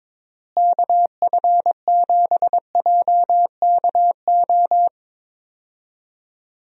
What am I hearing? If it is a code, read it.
Morse code, heard as a keyed tone, about 22 wpm, 700 Hz. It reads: KF7JKO
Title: USA Call Signs - 6 Characters - Repeated 2x - 22wpm